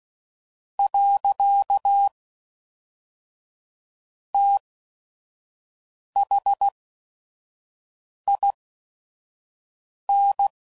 Morse code, heard as a keyed tone, about 16 wpm, spaced as for 5 wpm, 800 Hz. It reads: . THIN
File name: CW Code Groups @5WPM